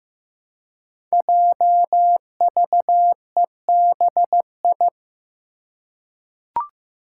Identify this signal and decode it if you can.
Morse code, heard as a keyed tone, about 15 wpm, 700 Hz. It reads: JVEBI
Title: Five Challenging Letter-Number Combinations 15wpm